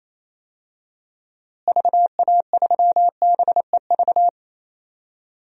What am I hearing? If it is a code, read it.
Morse code, heard as a keyed tone, about 28 wpm, 700 Hz. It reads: VA3BEV
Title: DXCC-100 Call Signs - 6 Characters - Repeated 3x - 28wpm